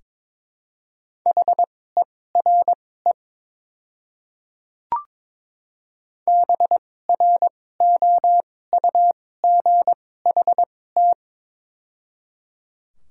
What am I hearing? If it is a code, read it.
Morse code, heard as a keyed tone, about 22 wpm, spaced as for 16 wpm, 700 Hz. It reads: HERE BROUGHT
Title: 2x Character Spacing - Top 500 Words - 22wpm